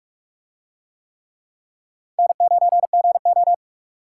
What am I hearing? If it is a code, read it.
Morse code, heard as a keyed tone, about 44 wpm, 700 Hz. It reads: N9GO